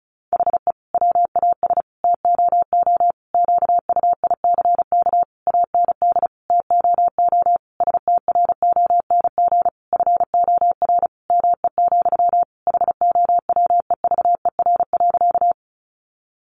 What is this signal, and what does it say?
Morse code, heard as a keyed tone, about 34 wpm, 700 Hz. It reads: HE WAS TOO QUICK AND TOO STRONG FOR ME, HOWEVER.
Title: Sentences from Top 600 Words 35wpm